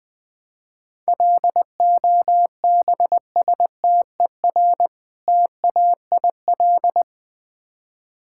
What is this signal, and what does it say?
Morse code, heard as a keyed tone, about 20 wpm, 700 Hz. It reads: LOBSTER TAIL
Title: Easy Words with Hints - 20wpm